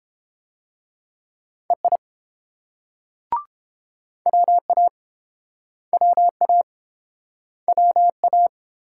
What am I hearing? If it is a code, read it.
Morse code, heard as a keyed tone, about 31 wpm, 700 Hz. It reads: EI WA WA WA